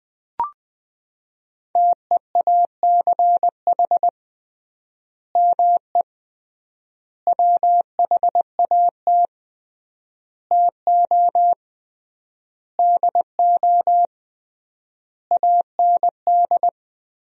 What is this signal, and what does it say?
Morse code, heard as a keyed tone, about 20 wpm, 700 Hz. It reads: TEACH ME WHAT TO DO AND
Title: 3x Word-Spacing - Sentences from Top 500 Words - 20wpm